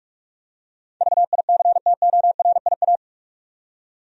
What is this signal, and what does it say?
Morse code, heard as a keyed tone, about 46 wpm, 700 Hz. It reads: VICTORIA
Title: USA Common Names - 45wpm